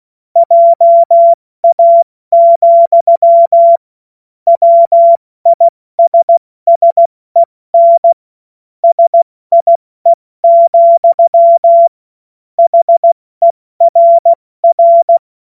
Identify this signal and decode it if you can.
Morse code, heard as a keyed tone, about 16 wpm, 680 Hz. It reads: JA, WISSEN SIE, HERR
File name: Morsekurs / Witze / Witz 19. April 2022 70 F80